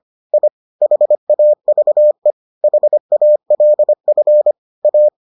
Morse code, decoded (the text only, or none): I HAVE HALF A